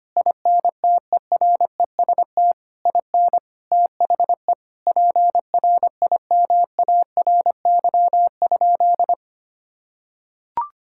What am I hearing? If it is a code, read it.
Morse code, heard as a keyed tone, about 25 wpm, 700 Hz. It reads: INTEREST IN THE PRIMARY?
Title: Sentences from Top 1000 Words - Review Part 5 - 25wpm